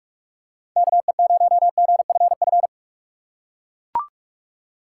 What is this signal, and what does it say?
Morse code, heard as a keyed tone, about 44 wpm, 700 Hz. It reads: KE0GFF
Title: USA Call Signs - 6 Characters - 6x Character Spacing - 45wpm